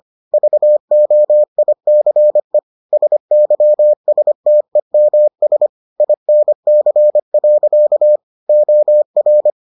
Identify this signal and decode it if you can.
Morse code, heard as a keyed tone, about 25 wpm, 600 Hz. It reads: VOICE SYSTEMS INC. OR